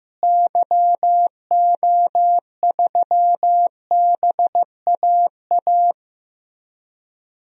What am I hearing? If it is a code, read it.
Morse code, heard as a keyed tone, about 16 wpm, 700 Hz. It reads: YO3BAA
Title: DXCC-100 Call Signs - 6 Characters - Repeated 3x - 15wpm